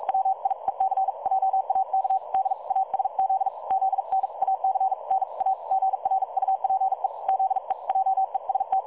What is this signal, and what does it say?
Morse code, heard as a keyed tone, about 45 wpm, spaced as for 39 wpm, 770 Hz. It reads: WIEC JAK DASZ ZNAC NA GRUPE ZE JEST
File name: dlazaawansowanych / nagraniazpasma / websdr recording 3565.0kHz